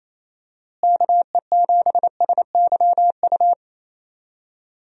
Morse code, heard as a keyed tone, about 27 wpm, 700 Hz. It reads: KE7SYU
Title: USA Call Signs - 6 Characters - Repeated 2x - 28wpm